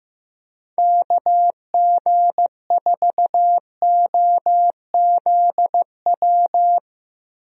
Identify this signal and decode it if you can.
Morse code, heard as a keyed tone, about 15 wpm, 700 Hz. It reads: KG4OZW
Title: USA Call Signs - 6 Characters - 15wpm